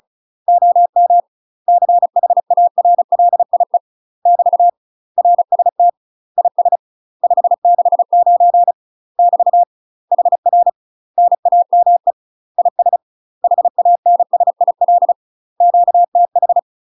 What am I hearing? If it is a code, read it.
Morse code, heard as a keyed tone, about 36 wpm, 700 Hz. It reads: OM CHARLIE <BT> RST IS 569 <BT> HR NAME IS HANSIL QTH
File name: UK MASTER QSOs TRAINING / 35WPM / UK MASTER QSOs 0033